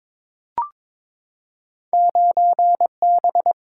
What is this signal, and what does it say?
Morse code, heard as a keyed tone, about 22 wpm, 700 Hz. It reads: 9B
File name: practice / 22wpm / Two Letter-Number Combinations - 22wpm